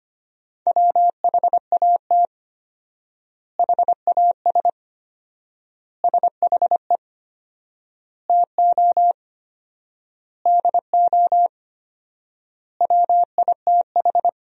Morse code, heard as a keyed tone, about 25 wpm, 700 Hz. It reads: WHAT HAS SHE TO DO WITH
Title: 4x Word-Spacing - Sentences from Top 100 Words - 25wpm